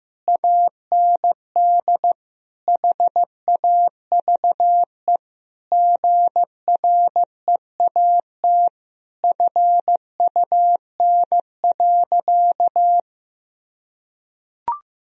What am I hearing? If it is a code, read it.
Morse code, heard as a keyed tone, about 15 wpm, 700 Hz. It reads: AND HAVE GREAT FUN.
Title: Sentences from Top 1000 Words - Review Part 4 - 15wpm